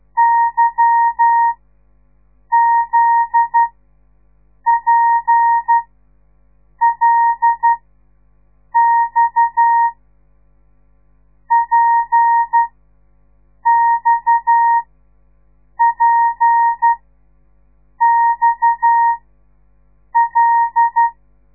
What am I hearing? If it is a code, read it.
Morse code, heard as a keyed tone, about 12 wpm, spaced as for 6 wpm, 930 Hz. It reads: YZPLX PXPXL
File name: radiomorningstar / morse / les 8